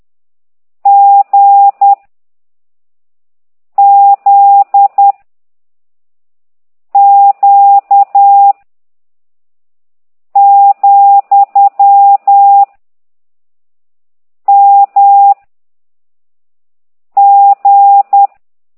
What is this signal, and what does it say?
Morse code, heard as a keyed tone, about 10 wpm, 800 Hz. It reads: G Z Q , M G